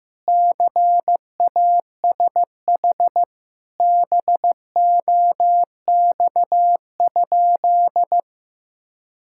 Morse code, heard as a keyed tone, about 16 wpm, 700 Hz. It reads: CASH BOX?
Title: Sentences from Top 1000 Words - Review Part 5 - 15wpm